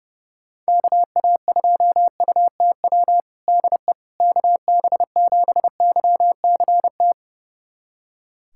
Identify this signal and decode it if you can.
Morse code, heard as a keyed tone, about 32 wpm, 700 Hz. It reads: KA2UTW DE KB7YCT
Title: USA Callsign DE Callsign 30wpm